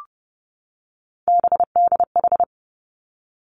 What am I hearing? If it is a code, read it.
Morse code, heard as a keyed tone, about 30 wpm, 700 Hz. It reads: BDH
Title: Three Challenging Letter-Number Combinations 30wpm